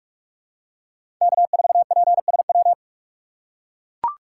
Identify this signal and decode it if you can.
Morse code, heard as a keyed tone, about 44 wpm, 700 Hz. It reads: K4PSW
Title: USA Call Signs - 5 Characters - 2x Character Spacing - 45wpm